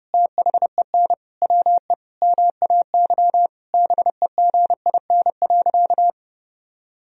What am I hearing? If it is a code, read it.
Morse code, heard as a keyed tone, about 30 wpm, 700 Hz. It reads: THEN WE MAY BEGIN.
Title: Sentences from Top 300 Words and Abbreviations 30wpm